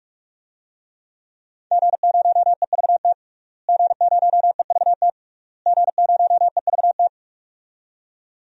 Morse code, heard as a keyed tone, about 44 wpm, 700 Hz. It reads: G0EVT G0EVT G0EVT
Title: DXCC-100 Call Signs - 5 Characters - Repeated 3x - 45wpm